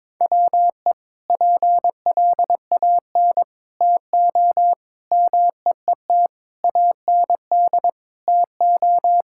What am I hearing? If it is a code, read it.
Morse code, heard as a keyed tone, about 22 wpm, 700 Hz. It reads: WE PLAN TO MEET AND TO
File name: practice / 22wpm / Sentences with 2-Word Phrases - 22wpm